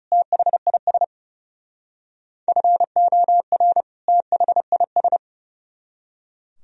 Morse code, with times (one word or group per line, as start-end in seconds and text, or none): 0.1-1.0 THIS
2.5-3.8 FOR
4.1-5.2 THIS